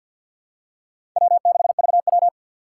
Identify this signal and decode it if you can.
Morse code, heard as a keyed tone, about 48 wpm, 700 Hz. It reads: W6VW
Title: USA Call Signs - 4 Characters - 6x Character Spacing - 50wpm